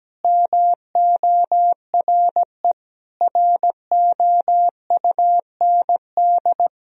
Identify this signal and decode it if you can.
Morse code, heard as a keyed tone, about 17 wpm, 700 Hz. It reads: MORE ROUND
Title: Sentences from Top 1000 Words - Review Part 4 - 17wpm